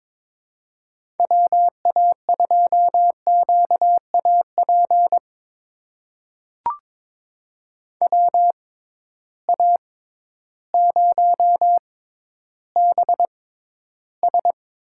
Morse code, heard as a keyed tone, about 22 wpm, spaced as for 12 wpm, 700 Hz. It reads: WA2QAP WA0BS
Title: USA Call Signs - 6 Characters - 6x Character Spacing - 22wpm